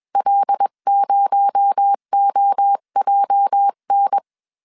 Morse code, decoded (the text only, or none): L0OJN